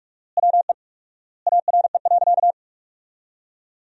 Morse code, heard as a keyed tone, about 43 wpm, 700 Hz. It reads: WE ARE.
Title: 4x Word-Spacing - Sentences from Top 200 Words - 45wpm